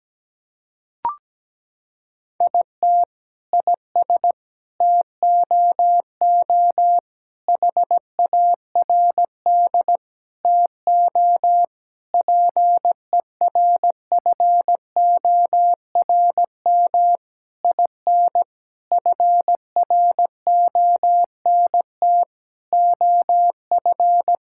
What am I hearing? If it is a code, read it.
Morse code, heard as a keyed tone, about 17 wpm, 700 Hz. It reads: IT IS TOO HARD TO PERFORM IN FRONT OF
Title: Sentences from Top 1000 Words - Review Part 5 - 17wpm